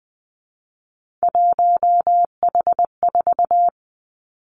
Morse code, heard as a keyed tone, about 20 wpm, 700 Hz. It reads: 1H4
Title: Three Challenging Letter-Number Combinations 20wpm